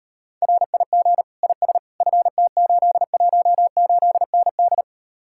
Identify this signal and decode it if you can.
Morse code, heard as a keyed tone, about 39 wpm, 700 Hz. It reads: RIG IS FT818ND